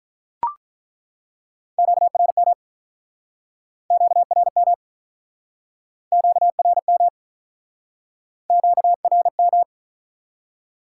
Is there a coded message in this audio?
Yes: Morse code from 1.8 to 9.6 s, heard as a keyed tone, about 48 wpm, 700 Hz. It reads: QRM QRM QRM QRM